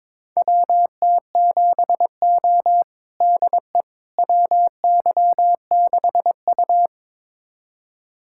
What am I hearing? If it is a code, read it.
Morse code, heard as a keyed tone, about 22 wpm, 700 Hz. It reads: WT7O DE WY6U
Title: USA Callsign DE Callsign - 4 Characters - 22wpm